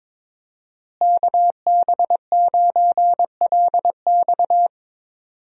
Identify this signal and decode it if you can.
Morse code, heard as a keyed tone, about 22 wpm, 700 Hz. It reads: KB9LX
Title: USA Call Signs - 5 Characters - Repeated 2x - 22wpm